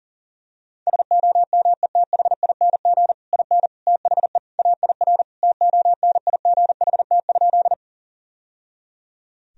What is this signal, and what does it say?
Morse code, heard as a keyed tone, about 40 wpm, 700 Hz. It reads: SOMETHING IN THE AIR TONIGHT?
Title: Sentences from Top 1000 Words - Review Part 1 - 40wpm